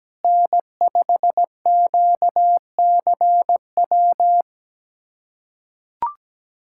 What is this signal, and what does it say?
Morse code, heard as a keyed tone, about 17 wpm, 700 Hz. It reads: N5QCW
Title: USA Callsign DE Callsign - 5 Characters - 17wpm